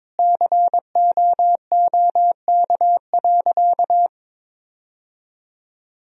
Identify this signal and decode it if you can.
Morse code, heard as a keyed tone, about 21 wpm, 700 Hz. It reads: COOK.
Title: Sentences from Top 1000 Words - Review Part 4 - 22wpm